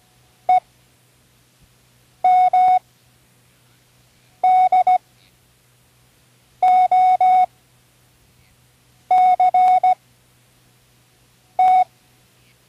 Morse code, heard as a keyed tone, about 17 wpm, spaced as for 5 wpm, 740 Hz. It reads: EMDOCT
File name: CWResources / Learning Morse Code (from Tune in the World) / 10 random